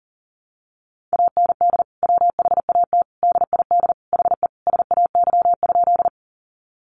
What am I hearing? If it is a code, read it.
Morse code, heard as a keyed tone, about 40 wpm, 700 Hz. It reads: AND WHAT DID HE SAY?
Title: Sentences from Top 200 Words 40wpm